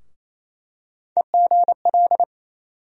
Morse code, heard as a keyed tone, about 28 wpm, 700 Hz. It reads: EGL
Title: USA Call Sign Postfixes - 3 Letters 28wpm